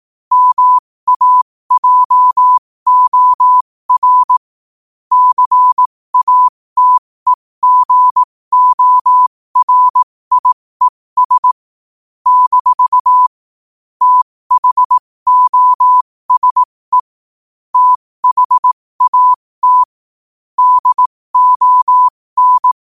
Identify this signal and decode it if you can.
Morse code, heard as a keyed tone, about 18 wpm, spaced as for 15 wpm, 1000 Hz. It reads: MAJOR CATEGORIES - THOSE THAT DON
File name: qotdfiles / 15wpm